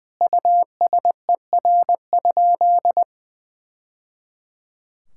Morse code, heard as a keyed tone, about 20 wpm, 700 Hz. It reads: USER?
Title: Sentences from Top 1000 Words - Review Part 5 - 20wpm